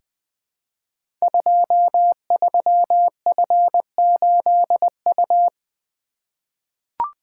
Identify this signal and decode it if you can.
Morse code, heard as a keyed tone, about 20 wpm, 700 Hz. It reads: 23F8U
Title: Code Groups - 5 Letter:Numbers - 20wpm